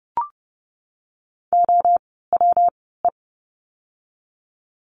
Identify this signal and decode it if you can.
Morse code, heard as a keyed tone, about 30 wpm, spaced as for 17 wpm, 700 Hz. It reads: OWE
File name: course / 30:10wpm / 115 Lesson 19f - Letter w - Words 30wpm